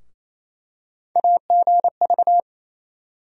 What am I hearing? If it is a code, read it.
Morse code, heard as a keyed tone, about 28 wpm, 700 Hz. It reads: AGV
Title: USA Call Sign Postfixes - 3 Letters 28wpm